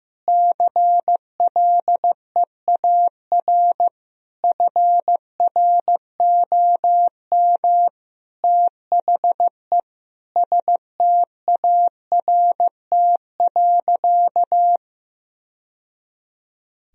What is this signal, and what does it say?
Morse code, heard as a keyed tone, about 15 wpm, 700 Hz. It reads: CLEAR FROM THE START.
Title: Sentences from Top 1000 Words - Review Part 1 - 15wpm